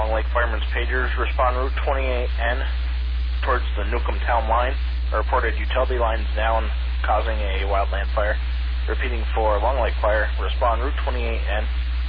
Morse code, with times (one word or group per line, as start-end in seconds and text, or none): none